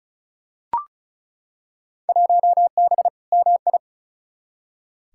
Morse code, heard as a keyed tone, about 36 wpm, 700 Hz. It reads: 1B MI
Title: Field Day Exchanges - 35wpm